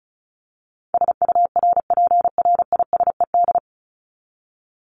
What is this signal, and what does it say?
Morse code, heard as a keyed tone, about 34 wpm, 700 Hz. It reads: SURPRISED
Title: Top 1501-1750 Words 35wpm